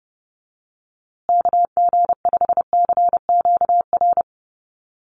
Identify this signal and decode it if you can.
Morse code, heard as a keyed tone, about 30 wpm, 700 Hz. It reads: KG5CQR
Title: USA Call Signs Repeated 3x 30wpm